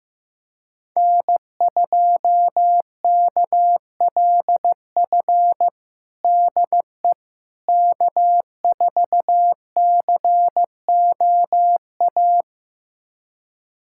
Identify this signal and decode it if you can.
Morse code, heard as a keyed tone, about 15 wpm, 700 Hz. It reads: N2KLF DE K4COA
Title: USA Callsign DE Callsign - 5 Characters - 15wpm